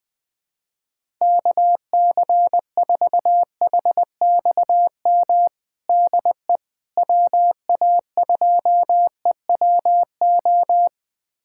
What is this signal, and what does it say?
Morse code, heard as a keyed tone, about 20 wpm, 700 Hz. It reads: KC4HXM DE WA2EWO
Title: USA Callsign DE Callsign 20wpm